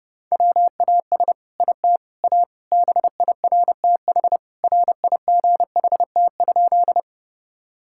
Morse code, heard as a keyed tone, about 30 wpm, 700 Hz. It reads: WAS IT A BIRTH RIGHT?
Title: Sentences from Top 1000 Words - Review Part 5 - 30wpm